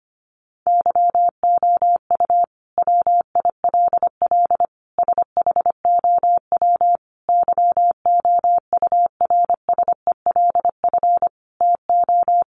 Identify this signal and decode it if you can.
Morse code, heard as a keyed tone, about 25 wpm, 700 Hz. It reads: YOU WILL SHOW YOURSELF TO
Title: Sentences from Top 600 Words 25wpm